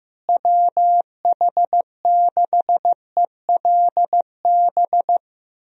Morse code, heard as a keyed tone, about 15 wpm, 700 Hz. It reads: WH6ELB
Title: USA Call Signs - 6 Characters - Repeated 3x - 15wpm